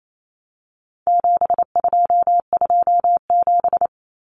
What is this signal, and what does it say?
Morse code, heard as a keyed tone, about 28 wpm, 700 Hz. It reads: 7227
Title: Common Cut Numbers - 4-Digit 28wpm